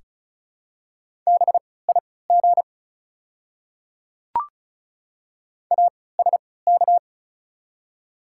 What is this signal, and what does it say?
Morse code, heard as a keyed tone, about 36 wpm, spaced as for 20 wpm, 690 Hz. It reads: BIG ASK
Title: 3x Character Spacing - Top 200 Words - 35wpm